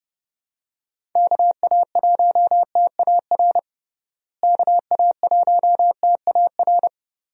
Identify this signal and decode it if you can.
Morse code, heard as a keyed tone, about 30 wpm, 700 Hz. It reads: KA1TAR KA1TAR
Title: USA Call Signs - 6 Characters - Repeated 2x - 30wpm